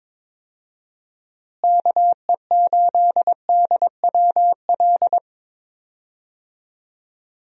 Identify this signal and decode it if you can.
Morse code, heard as a keyed tone, about 22 wpm, 700 Hz. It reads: KE8DWL
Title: USA Call Signs - 6 Characters - Repeated 3x - 22wpm